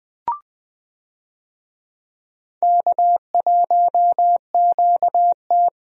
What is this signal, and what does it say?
Morse code, heard as a keyed tone, about 20 wpm, 700 Hz. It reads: K1QT
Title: USA Call Signs - 4 Characters - Repeated 2x - 20wpm